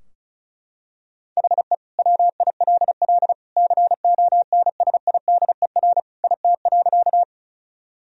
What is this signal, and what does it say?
Morse code, heard as a keyed tone, about 34 wpm, 700 Hz. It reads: HE WILL CONSIDER IT.
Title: Sentences from Top 1000 Words - Review Part 4 - 35wpm